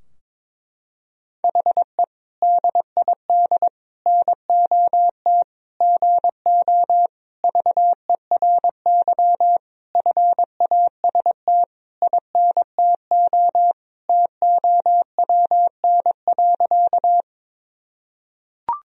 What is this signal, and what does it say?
Morse code, heard as a keyed tone, about 22 wpm, 700 Hz. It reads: HE DID NOT GO VERY FAST INTO TOWN.
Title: Sentences from Top 1000 Words - Review Part 2 - 22wpm